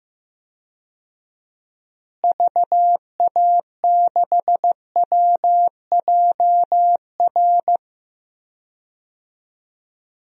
Morse code, heard as a keyed tone, about 16 wpm, 700 Hz. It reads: VA6WJR
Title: DXCC-100 Call Signs - 6 Characters - Repeated 2x - 15wpm